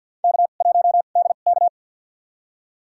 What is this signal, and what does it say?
Morse code, heard as a keyed tone, about 51 wpm, spaced as for 36 wpm, 700 Hz. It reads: K1DK
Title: USA Call Signs - 4 Characters - 2x Character Spacing - 50wpm